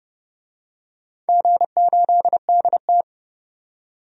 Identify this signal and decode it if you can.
Morse code, heard as a keyed tone, about 30 wpm, 700 Hz. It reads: G8DT
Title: DXCC-100 Call Signs - 4 Characters - 30wpm